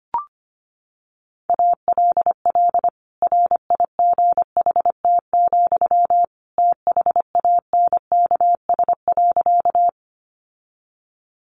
Morse code, heard as a keyed tone, about 25 wpm, 700 Hz. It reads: ALL RIGHT, THANKS.
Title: Sentences from Top 600 Words 25wpm